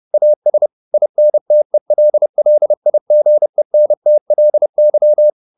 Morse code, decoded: AS INTELLIGENTLY

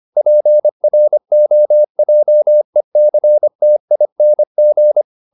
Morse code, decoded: PROJECTING

